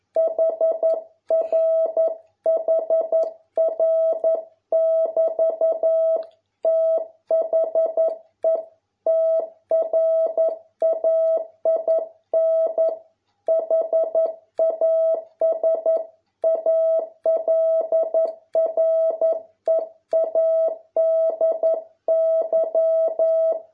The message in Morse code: HR HR <BT> THE TRAIN HAS ALREADY

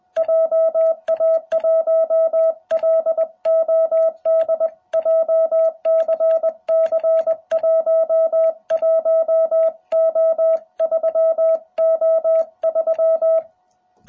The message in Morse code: JA1LOD JCC11O3O3

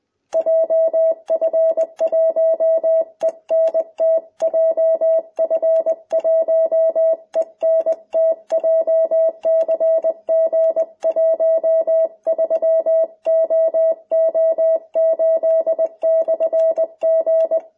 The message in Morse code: JF1ENT JF1ENT JCG13OO8/G

